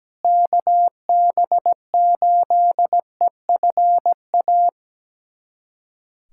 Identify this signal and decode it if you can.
Morse code, heard as a keyed tone, about 17 wpm, 700 Hz. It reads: KB8EFA
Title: USA Callsign DE Callsign - 6 Characters - 17wpm